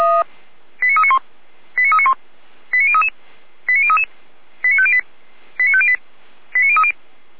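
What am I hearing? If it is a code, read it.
Five-tone selective call (ZVEI), heard as quick runs of five tones, tones of about 70 ms. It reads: 89391 89391 89033 89033 89508 89508 89030
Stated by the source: ZVEI five-tone selective call received via SDR.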